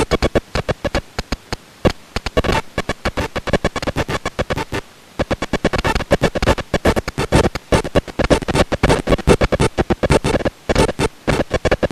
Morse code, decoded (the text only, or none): none